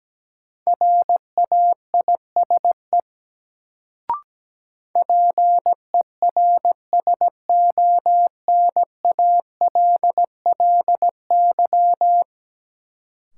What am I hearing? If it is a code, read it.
Morse code, heard as a keyed tone, about 17 wpm, 700 Hz. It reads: RAISE PERSONALLY
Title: Sentences from Top 1400 Words - New Word Drill - 17wpm